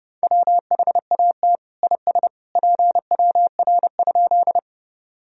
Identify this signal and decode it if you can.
Morse code, heard as a keyed tone, about 28 wpm, 700 Hz. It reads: WHAT IS PWR?
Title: Sentences from Top 500 Words and Abbreviations 30wpm